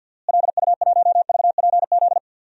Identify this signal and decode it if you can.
Morse code, heard as a keyed tone, about 52 wpm, 700 Hz. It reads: LU1VPZ
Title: DXCC-100 Call Signs - 6 Characters - Repeated 2x - 50wpm